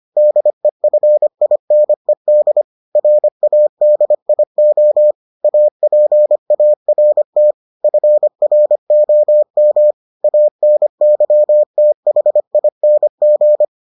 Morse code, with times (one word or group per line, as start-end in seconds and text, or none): 0.2-2.6 DEFINED
3.0-5.1 RADIO
5.4-7.5 APART
7.8-9.9 FROM
10.2-13.7 ANYTHING